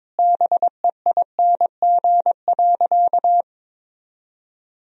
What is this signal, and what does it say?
Morse code, heard as a keyed tone, about 22 wpm, 700 Hz. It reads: BEING.
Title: Sentences from Top 1000 Words - Review Part 4 - 22wpm